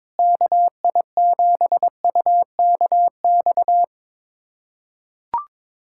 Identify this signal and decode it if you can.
Morse code, heard as a keyed tone, about 22 wpm, 700 Hz. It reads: KI7UKX
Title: USA Callsign DE Callsign - 6 Characters - 22wpm